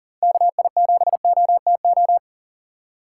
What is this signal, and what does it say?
Morse code, heard as a keyed tone, about 40 wpm, 700 Hz. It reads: KI7OTO
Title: USA Call Signs - 6 Characters - 3x Character Spacing - 40wpm